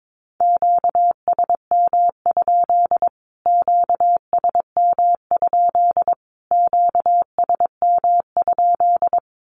Morse code, heard as a keyed tone, about 22 wpm, 700 Hz. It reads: QSM? QSM? QSM?
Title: Q-Codes Repeated 3x 22wpm